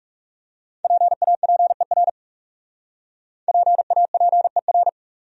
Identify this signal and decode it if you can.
Morse code, heard as a keyed tone, about 40 wpm, 700 Hz. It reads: PAPER PAPER